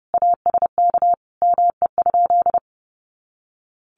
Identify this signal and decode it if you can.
Morse code, heard as a keyed tone, about 30 wpm, 700 Hz. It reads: ASK ME?
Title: Sentences from Top 200 Words 30wpm